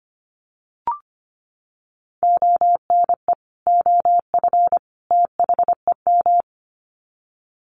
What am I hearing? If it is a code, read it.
Morse code, heard as a keyed tone, about 25 wpm, 700 Hz. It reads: ONE OF THEM
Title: Sets of 3 Words 25wpm